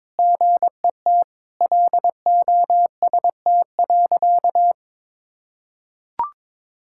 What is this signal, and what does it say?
Morse code, heard as a keyed tone, about 22 wpm, 700 Hz. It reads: GET LOST.
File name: practice / 22wpm / Sentences with 3-Word Phrases - 22wpm